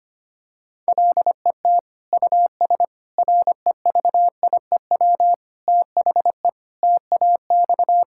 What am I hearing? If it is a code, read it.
Morse code, heard as a keyed tone, about 25 wpm, 700 Hz. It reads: LET US REVIEW THE TAX
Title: Sentences from Top 1000 Words - Review Part 6 - 25wpm